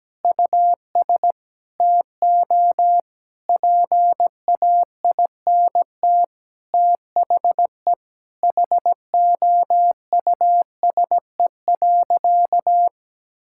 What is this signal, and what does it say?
Morse code, heard as a keyed tone, about 17 wpm, 700 Hz. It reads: US TO PAINT THE HOUSE.
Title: Sentences from Top 1000 Words - Review Part 6 - 17wpm